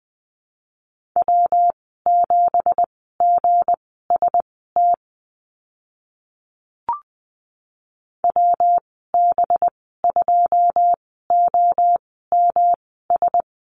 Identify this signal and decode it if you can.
Morse code, heard as a keyed tone, about 20 wpm, spaced as for 15 wpm, 700 Hz. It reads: W7GST WB2OMS
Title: 193 Lesson 31g - Letter v - Call Signs 20wpm